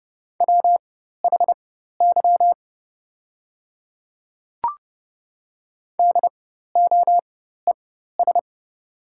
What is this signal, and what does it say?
Morse code, heard as a keyed tone, about 30 wpm, spaced as for 14 wpm, 700 Hz. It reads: WHY DOES
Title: 4x Character Spacing - Top 500 Words - 30wpm